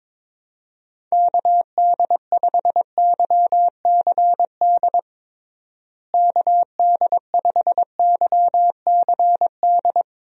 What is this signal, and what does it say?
Morse code, heard as a keyed tone, about 22 wpm, 700 Hz. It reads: KD5YCD KD5YCD
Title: USA Call Signs - 6 Characters - Repeated 2x - 22wpm